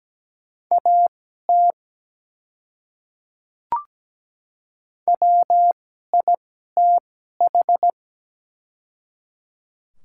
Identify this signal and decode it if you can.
Morse code, heard as a keyed tone, about 17 wpm, spaced as for 12 wpm, 700 Hz. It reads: AT WITH
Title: 2x Character Spacing - Top 200 Words - 17wpm